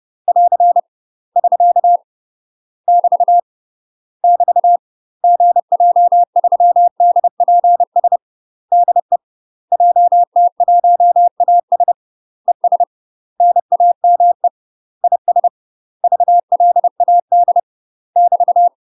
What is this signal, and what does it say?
Morse code, heard as a keyed tone, about 30 wpm, 700 Hz. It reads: <AR> <SK> <BT> <BT> GJ3DPS DE JT1AS ES NAME IS VLAD <BT>
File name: UK MASTER QSOs TRAINING / 30WPM / UK MASTER QSOs 0022